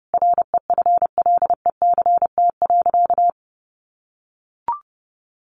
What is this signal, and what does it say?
Morse code, heard as a keyed tone, about 30 wpm, 700 Hz. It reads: REFLECT.
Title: Sentences from Top 1500 Words 30wpm